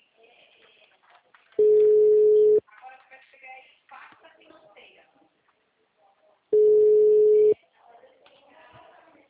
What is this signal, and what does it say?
Ringback tone (Europe, 425 Hz).